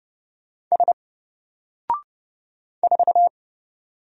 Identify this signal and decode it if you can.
Morse code, heard as a keyed tone, about 30 wpm, 700 Hz. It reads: S 4
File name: Single-Letter-Number-30wpm